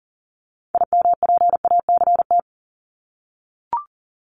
Morse code, heard as a keyed tone, about 40 wpm, 700 Hz. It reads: IMPACT
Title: Top 751-1000 Words 40wpm